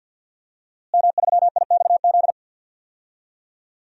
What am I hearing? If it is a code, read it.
Morse code, heard as a keyed tone, about 52 wpm, 700 Hz. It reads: M3IXZ
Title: DXCC-100 Call Signs - 5 Characters - Repeated 3x - 50wpm